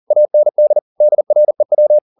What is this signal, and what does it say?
Morse code, heard as a keyed tone, about 40 wpm, 600 Hz. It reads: AND DREW
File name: the old man and the sea 2 40wpm